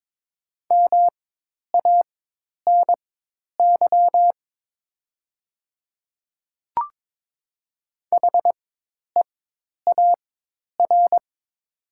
Morse code, heard as a keyed tone, about 22 wpm, spaced as for 10 wpm, 700 Hz. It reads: MANY HEAR
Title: 4x Character Spacing - Top 300 Words - 22wpm